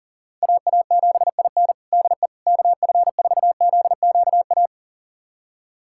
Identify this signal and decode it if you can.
Morse code, heard as a keyed tone, about 40 wpm, 700 Hz. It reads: AA7IN DE KF4ZQA
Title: USA Callsign DE Callsign 40wpm